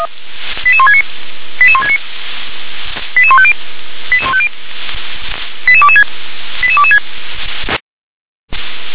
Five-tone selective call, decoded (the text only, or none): 88169 88169 88150 88150 88226 88226